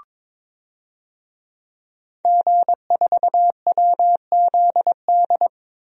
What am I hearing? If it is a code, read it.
Morse code, heard as a keyed tone, about 21 wpm, 700 Hz. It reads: G4WZD
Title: DXCC-100 Call Signs - 5 Characters - Repeated 2x - 22wpm